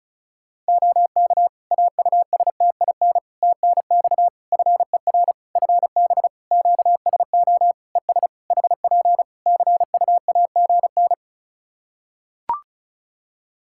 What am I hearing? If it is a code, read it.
Morse code, heard as a keyed tone, about 36 wpm, 700 Hz. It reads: OK AUSTIN TNX FER FB QSO ES HP CUAGN
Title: Practice QSO - Exchange 4 - Ending - 35wpm